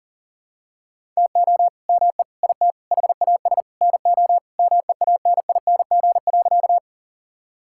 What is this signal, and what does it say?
Morse code, heard as a keyed tone, about 40 wpm, 700 Hz. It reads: TO ME IT HAS NO MEANING.
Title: Sentences from Top 1000 Words - Review Part 6 - 40wpm